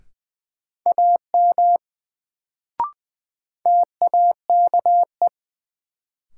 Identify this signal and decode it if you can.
Morse code, heard as a keyed tone, about 20 wpm, 700 Hz. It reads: AM TAKE
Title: Sets of 2 Words - New Word Drill - 20wpm